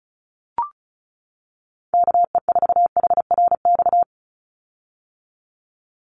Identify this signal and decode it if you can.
Morse code, heard as a keyed tone, about 34 wpm, 700 Hz. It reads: KE4HRX
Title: USA Call Signs 35wpm